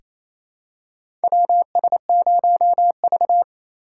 Morse code, heard as a keyed tone, about 27 wpm, 700 Hz. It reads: WS0V
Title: USA Call Signs - 3x Character Spacing - 28wpm